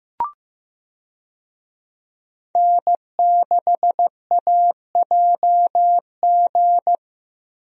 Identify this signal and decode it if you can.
Morse code, heard as a keyed tone, about 15 wpm, 700 Hz. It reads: N6AJG
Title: USA Call Signs - 5 Characters - Repeated 2x - 15wpm